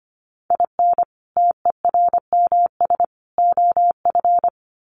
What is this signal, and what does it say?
Morse code, heard as a keyed tone, about 25 wpm, 700 Hz. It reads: IN TERMS OF
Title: Sets of 3 Words 25wpm